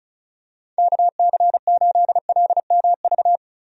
Morse code, heard as a keyed tone, about 36 wpm, 700 Hz. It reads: KC8LMV
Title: USA Call Signs - 6 Characters - Repeated 2x - 35wpm